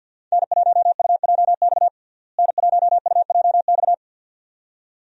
Morse code, heard as a keyed tone, about 48 wpm, 700 Hz. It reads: N1UJX N1UJX
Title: USA Call Signs - 5 Characters - Repeated 2x - 50wpm